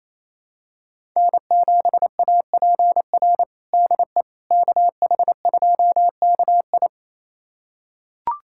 Morse code, heard as a keyed tone, about 27 wpm, 700 Hz. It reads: N7APR DE KH2KI
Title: USA Callsign DE Callsign - 5 Characters - 28wpm